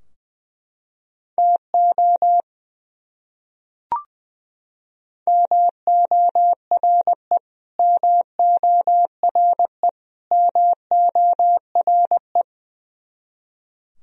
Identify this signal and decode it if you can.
Morse code, heard as a keyed tone, about 20 wpm, 700 Hz. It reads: TO MORE MORE MORE
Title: Top 100 Words Repeated 20wpm